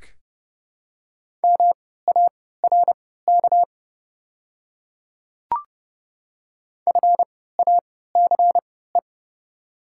Morse code, heard as a keyed tone, about 30 wpm, spaced as for 17 wpm, 690 Hz. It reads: MARK FACE
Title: 3x Character Spacing - Top 500 Words - 30wpm